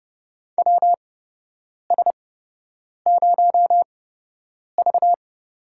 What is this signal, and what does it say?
Morse code, heard as a keyed tone, about 30 wpm, spaced as for 8 wpm, 700 Hz. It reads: WS0V
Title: USA Call Signs - 4 Characters - 8x Character Spacing - 30wpm